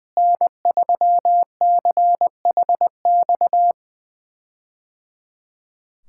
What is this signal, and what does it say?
Morse code, heard as a keyed tone, about 20 wpm, 700 Hz. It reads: N3CHX
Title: K1USN SST QSO Practice - 20wpm